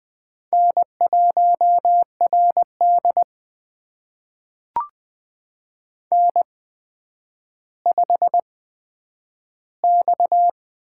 Morse code, heard as a keyed tone, about 21 wpm, spaced as for 10 wpm, 700 Hz. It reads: N1RD N5X
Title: USA Call Signs - 8x Character Spacing - 20wpm